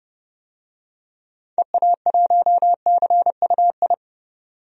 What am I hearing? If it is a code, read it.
Morse code, heard as a keyed tone, about 30 wpm, 700 Hz. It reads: EA1CUI
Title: DXCC-100 Call Signs - 6 Characters - 30wpm